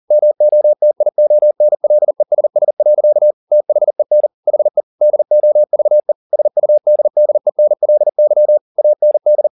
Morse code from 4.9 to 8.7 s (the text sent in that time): DOVE SUDDENLY